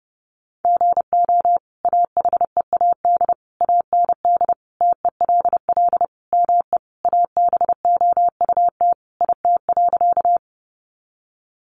Morse code, heard as a keyed tone, about 30 wpm, 700 Hz. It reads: GO AHEAD AND TELL ME ABOUT IT.
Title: Sentences from Top 1100 Words 30wpm